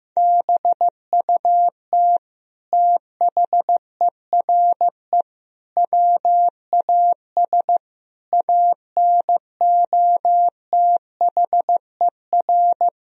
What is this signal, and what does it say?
Morse code, heard as a keyed tone, about 15 wpm, 700 Hz. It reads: BUT THERE WAS ANOTHER
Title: Sentences from Top 1000 Words - Review Part 2 - 15wpm